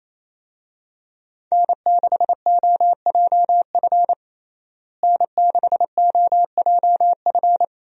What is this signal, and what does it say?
Morse code, heard as a keyed tone, about 28 wpm, 700 Hz. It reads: N6OJF N6OJF
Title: USA Call Signs - 5 Characters - Repeated 2x - 28wpm